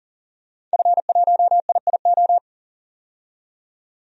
Morse code, heard as a keyed tone, about 40 wpm, 700 Hz. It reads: F1IIO
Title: DXCC-100 Call Signs - 5 Characters - 40wpm